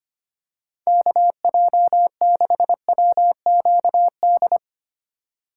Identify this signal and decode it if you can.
Morse code, heard as a keyed tone, about 25 wpm, 700 Hz. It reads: KJ6WQD
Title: USA Call Signs - 6x Character Spacing - 25wpm